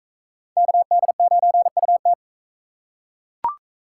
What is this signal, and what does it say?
Morse code, heard as a keyed tone, about 42 wpm, 700 Hz. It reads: KD9UT